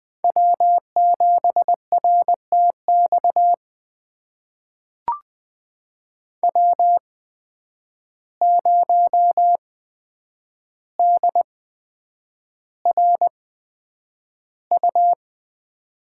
Morse code, heard as a keyed tone, about 20 wpm, spaced as for 9 wpm, 700 Hz. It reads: W7RTX W0DRU
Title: USA Call Signs - 5 Characters - 8x Character Spacing - 20wpm